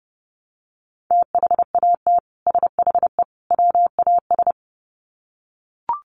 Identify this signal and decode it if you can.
Morse code, heard as a keyed tone, about 30 wpm, 700 Hz. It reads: THAT SHE WAS
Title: Sets of 3 Words 30wpm